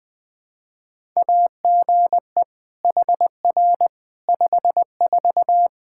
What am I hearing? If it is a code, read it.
Morse code, heard as a keyed tone, about 20 wpm, 700 Hz. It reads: AGE HR 54
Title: Practice QSO - Exchange 3 - More Info - 20wpm